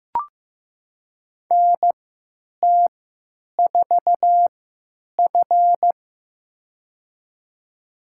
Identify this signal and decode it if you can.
Morse code, heard as a keyed tone, about 15 wpm, spaced as for 9 wpm, 700 Hz. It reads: NT4F
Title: USA Call Signs - 4 Characters - 3x Character Spacing - 15wpm